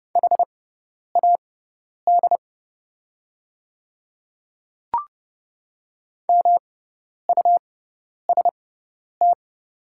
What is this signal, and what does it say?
Morse code, heard as a keyed tone, about 30 wpm, spaced as for 10 wpm, 700 Hz. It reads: HAD MUST
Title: 6x Character Spacing - Top 300 Words - 30wpm